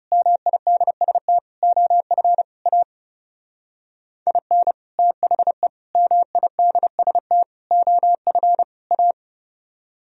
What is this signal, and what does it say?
Morse code, heard as a keyed tone, about 36 wpm, 700 Hz. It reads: MIDST OF A IN THE MIDST OF A
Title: Speed-Racing - Sets of 5 Words 30wpm